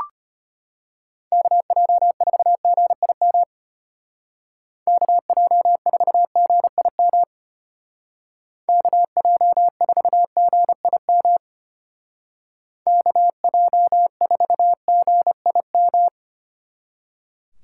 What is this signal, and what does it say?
Morse code, heard as a keyed tone, about 30 wpm, 700 Hz. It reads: KJ4GIM KJ4GIM KJ4GIM KJ4GIM
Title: Speed Racing - USA Call Signs 25wpm